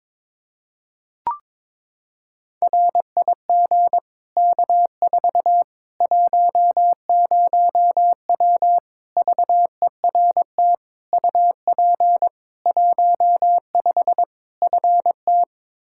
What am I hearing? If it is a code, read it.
Morse code, heard as a keyed tone, about 22 wpm, 700 Hz. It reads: RIG K4 10W VERT UP 15 FT
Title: Practice QSO - Exchange 2 - Info - 22wpm